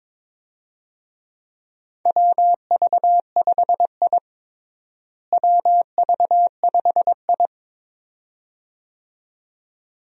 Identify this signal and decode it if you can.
Morse code, heard as a keyed tone, about 22 wpm, 700 Hz. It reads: WV5I WV5I
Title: USA Call Signs - 4 Characters - Repeated 2x - 22wpm